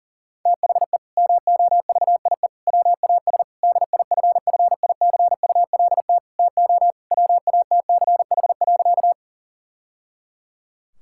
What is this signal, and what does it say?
Morse code, heard as a keyed tone, about 40 wpm, 700 Hz. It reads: THE MOVIE WAS DIFFICULT TO WATCH.
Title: Sentences from Top 1000 Words - Review Part 5 - 40wpm